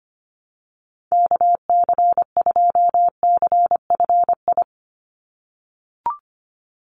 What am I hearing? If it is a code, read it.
Morse code, heard as a keyed tone, about 25 wpm, 700 Hz. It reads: KC2CFI
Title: USA Call Signs 25wpm